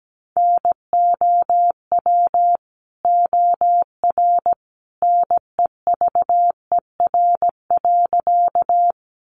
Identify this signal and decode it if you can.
Morse code, heard as a keyed tone, about 17 wpm, 700 Hz. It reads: NOW OR NEVER.
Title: Sentences from Top 300 Words 17wpm